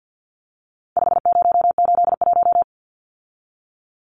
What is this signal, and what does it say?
Morse code, heard as a keyed tone, about 50 wpm, 700 Hz. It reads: 5081
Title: Numbers - 4-Digit 50wpm